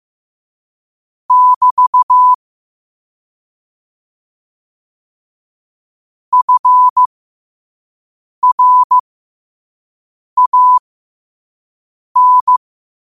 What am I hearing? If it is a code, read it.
Morse code, heard as a keyed tone, about 15 wpm, spaced as for 5 wpm, 1000 Hz. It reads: <BT> FRAN
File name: qotdfiles / 5wpm